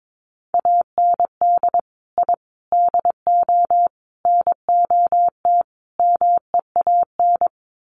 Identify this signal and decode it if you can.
Morse code, heard as a keyed tone, about 22 wpm, 700 Hz. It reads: AND I DO NOT MEAN
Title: Sets of 5 Words Encore2 22wpm